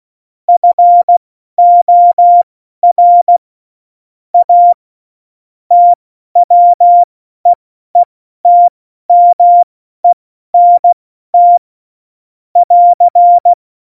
Morse code, heard as a keyed tone, about 16 wpm, spaced as for 12 wpm, 700 Hz. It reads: FOR A TWEETMENT <AR>
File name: Morsekurs / Witze